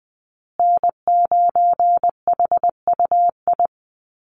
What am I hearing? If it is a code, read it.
Morse code, heard as a keyed tone, about 20 wpm, 700 Hz. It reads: N9HUI
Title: USA Call Signs Repeated 3x 20wpm